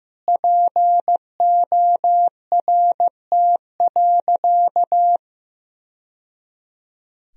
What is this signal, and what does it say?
Morse code, heard as a keyed tone, about 16 wpm, 700 Hz. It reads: PORT.
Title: Sentences from Top 1000 Words - Review Part 1 - 15wpm